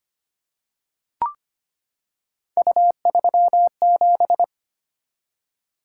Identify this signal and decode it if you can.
Morse code, heard as a keyed tone, about 25 wpm, 700 Hz. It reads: U37
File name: Three Letter-Number Combinations - 25wpm